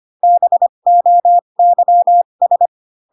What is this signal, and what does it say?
Morse code, heard as a keyed tone, about 25 wpm, spaced as for 22 wpm, 700 Hz. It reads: BOYS
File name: Pinocchio / Pinocchio ch6 18 25 ik1hge